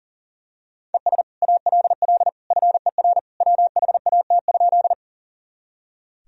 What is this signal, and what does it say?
Morse code, heard as a keyed tone, about 40 wpm, 700 Hz. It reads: ES ALL FER WHAT?
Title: Sentences from Top 100 Words and Abbreviations 40wpm